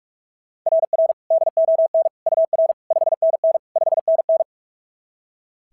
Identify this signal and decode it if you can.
Morse code, heard as a keyed tone, about 44 wpm, 690 Hz. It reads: RR DON UR 5NN 5NN
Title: SOTA QSO Practice 45wpm